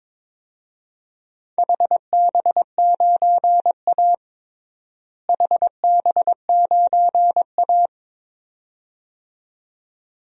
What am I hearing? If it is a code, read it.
Morse code, heard as a keyed tone, about 22 wpm, 700 Hz. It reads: HB9A HB9A
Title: DXCC-100 Call Signs - 4 Characters - Repeated 2x - 22wpm